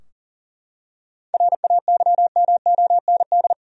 Morse code, heard as a keyed tone, about 40 wpm, 700 Hz. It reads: RAYMOND
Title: USA Common Names - 40wpm